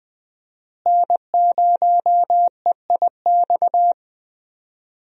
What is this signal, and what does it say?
Morse code, heard as a keyed tone, about 20 wpm, 700 Hz. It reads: N0EIX
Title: USA Call Signs - 5 Characters - Repeated 3x - 20wpm